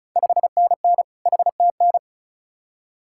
Morse code, heard as a keyed tone, about 34 wpm, 700 Hz. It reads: 5NN HTN
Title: Ontario QSO Party Exchange 35wpm